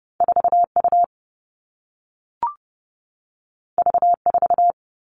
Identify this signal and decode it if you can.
Morse code, heard as a keyed tone, about 30 wpm, 700 Hz. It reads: 4 U V 4
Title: Two Challenging Letter-Number Combinations 30wpm